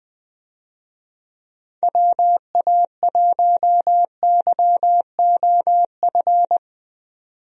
Morse code, heard as a keyed tone, about 20 wpm, 700 Hz. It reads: WA1YOF
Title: USA Call Signs - 6 Characters - 20wpm